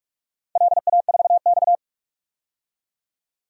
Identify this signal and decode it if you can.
Morse code, heard as a keyed tone, about 44 wpm, 700 Hz. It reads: LA4X
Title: DXCC-100 Call Signs - 4 Characters - 45wpm